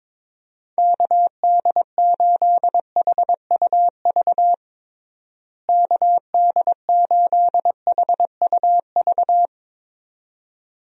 Morse code, heard as a keyed tone, about 22 wpm, 700 Hz. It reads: KD8HUV KD8HUV
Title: USA Call Signs - 6 Characters - Repeated 2x - 22wpm